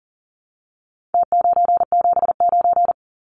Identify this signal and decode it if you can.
Morse code, heard as a keyed tone, about 40 wpm, 700 Hz. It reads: T979
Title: Numbers - Contest Style - 4-Digit 40wpm